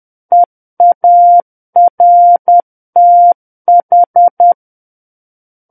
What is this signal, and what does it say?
Morse code, heard as a keyed tone, about 10 wpm, 700 Hz. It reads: EARTH
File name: Book One / 01-Chapter 1 - 10 WPM